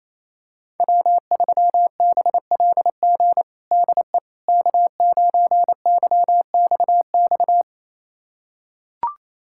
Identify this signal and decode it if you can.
Morse code, heard as a keyed tone, about 27 wpm, 700 Hz. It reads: W3BLG DE K9YXX
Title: USA Callsign DE Callsign - 5 Characters - 28wpm